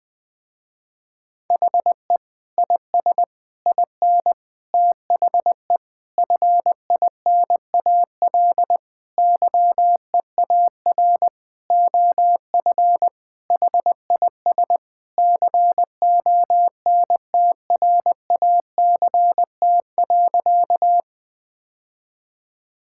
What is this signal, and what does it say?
Morse code, heard as a keyed tone, about 21 wpm, 700 Hz. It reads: HE IS IN THE FINAL YEAR OF HIS CONTRACT.
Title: Sentences with 8-Word Phrases - Spoken Beginning - 20wpm